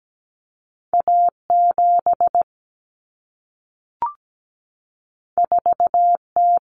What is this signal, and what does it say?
Morse code, heard as a keyed tone, about 17 wpm, 700 Hz. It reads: A7 4T
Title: Common Cut Numbers - 2-Digit 17wpm